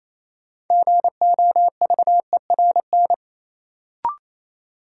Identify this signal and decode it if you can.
Morse code, heard as a keyed tone, about 28 wpm, 700 Hz. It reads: GOVERN
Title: Sentences from Top 1000 Words - Review Part 2 - New Word Drill - 28wpm